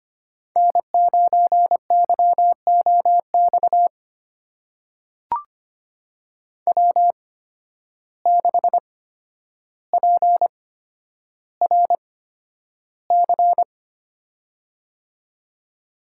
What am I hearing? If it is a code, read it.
Morse code, heard as a keyed tone, about 25 wpm, spaced as for 11 wpm, 700 Hz. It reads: N9YOX W6PRC
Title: USA Call Signs - 5 Characters - 8x Character Spacing - 25wpm